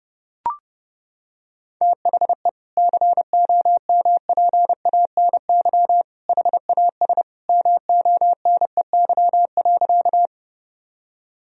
Morse code, heard as a keyed tone, about 30 wpm, 700 Hz. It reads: THE COMPANY HAS MONEY.
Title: Sentences from Top 1000 Words - Review Part 3 - 30wpm